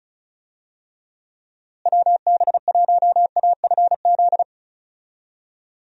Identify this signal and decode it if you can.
Morse code, heard as a keyed tone, about 36 wpm, 700 Hz. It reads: WB1AFZ
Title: USA Call Signs - 6 Characters - Repeated 3x - 35wpm